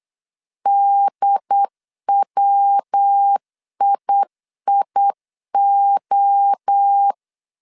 Morse code, heard as a keyed tone, about 8 wpm, 800 Hz. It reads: DWIIO